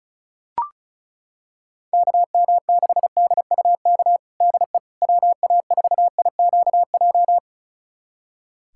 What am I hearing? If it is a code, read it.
Morse code, heard as a keyed tone, about 36 wpm, 700 Hz. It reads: KM6DUK DE WA4IQJ
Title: USA Callsign DE Callsign - 6 Characters - 35wpm